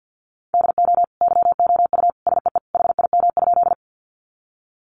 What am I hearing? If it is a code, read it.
Morse code, heard as a keyed tone, about 50 wpm, 700 Hz. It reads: DO YOU SEE HIM?
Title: Sentences from Top 100 Words 50wpm